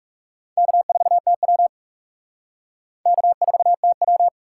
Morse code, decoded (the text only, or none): K4TW K4TW